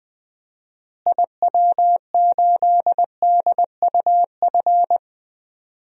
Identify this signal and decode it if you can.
Morse code, heard as a keyed tone, about 20 wpm, 700 Hz. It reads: IW8DUF
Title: DXCC-100 Call Signs - 6 Characters - Repeated 2x - 20wpm